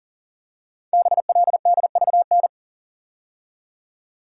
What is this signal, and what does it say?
Morse code, heard as a keyed tone, about 41 wpm, 700 Hz. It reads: BLDVN
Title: Code Groups - 5 Letter:Numbers - 40wpm